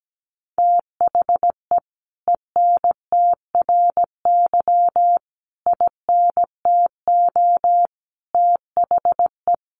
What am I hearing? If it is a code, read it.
Morse code, heard as a keyed tone, about 17 wpm, 700 Hz. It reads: THE ENTRY INTO THE
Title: Sentences from Top 1500 Words 17wpm